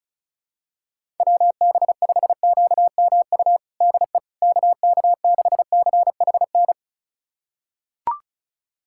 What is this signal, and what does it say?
Morse code, heard as a keyed tone, about 36 wpm, 700 Hz. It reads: WB5QMU DE KK6CHN
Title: USA Callsign DE Callsign 35wpm